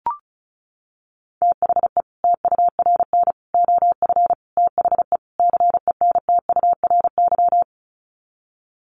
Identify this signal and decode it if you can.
Morse code, heard as a keyed tone, about 34 wpm, 700 Hz. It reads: THE TURN OF THE CENTURY
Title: Sets of 5 Words 35wpm